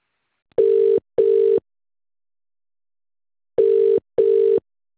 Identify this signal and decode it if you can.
Ringback tone (UK), 400 + 450 Hz.